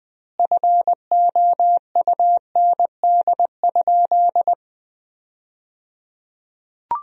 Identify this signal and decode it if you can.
Morse code, heard as a keyed tone, about 21 wpm, 700 Hz. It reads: FOUND?
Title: 3x Word-Spacing - Sentences from Top 300 Words - 20wpm